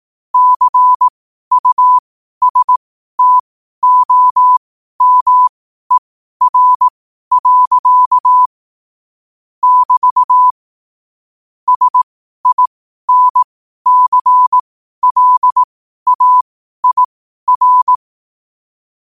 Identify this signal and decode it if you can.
Morse code, heard as a keyed tone, about 18 wpm, spaced as for 12 wpm, 1000 Hz. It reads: CUSTOMER. <BT> SINCLAIR
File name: qotdfiles / 12wpm